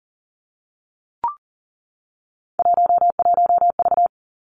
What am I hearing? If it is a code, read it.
Morse code, heard as a keyed tone, about 39 wpm, 700 Hz. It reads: 11V
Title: Three Challenging Letter-Number Combinations 40wpm